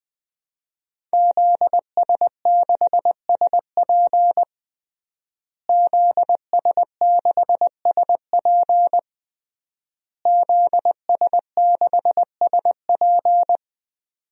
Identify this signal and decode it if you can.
Morse code, heard as a keyed tone, about 20 wpm, 700 Hz. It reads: ZS6SP ZS6SP ZS6SP
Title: DXCC-100 Call Signs - 5 Characters - Repeated 3x - 20wpm